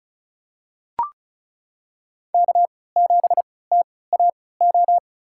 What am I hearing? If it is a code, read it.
Morse code, heard as a keyed tone, about 36 wpm, spaced as for 20 wpm, 700 Hz. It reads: K7TAO
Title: USA Call Signs - 5 Characters - 3x Character Spacing - 35wpm